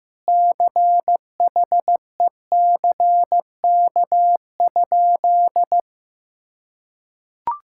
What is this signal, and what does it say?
Morse code, heard as a keyed tone, about 16 wpm, 700 Hz. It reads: CHECK?
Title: Sentences from Top 1000 Words - Review Part 6 - 15wpm